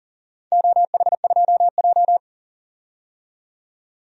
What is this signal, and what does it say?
Morse code, heard as a keyed tone, about 40 wpm, 700 Hz. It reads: OH2J
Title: DXCC-100 Call Signs - 4 Characters - 40wpm